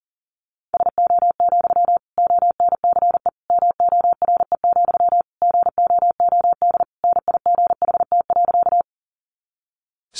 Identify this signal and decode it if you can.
Morse code, heard as a keyed tone, about 40 wpm, 700 Hz. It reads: SO, ONCE MORE, GOOD NIGHT.
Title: Sentences from Top 300 Words 40wpm